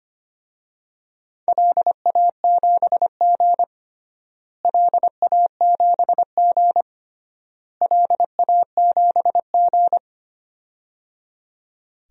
Morse code, heard as a keyed tone, about 25 wpm, 700 Hz. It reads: LA7G LA7G LA7G
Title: DXCC-100 Call Signs - 4 Characters - Repeated 3x - 25wpm